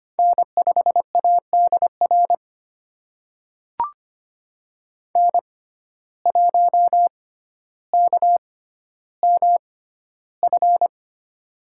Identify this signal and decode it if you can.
Morse code, heard as a keyed tone, about 25 wpm, spaced as for 13 wpm, 700 Hz. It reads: N5ADR N1KMF
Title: USA Call Signs - 5 Characters - 6x Character Spacing - 25wpm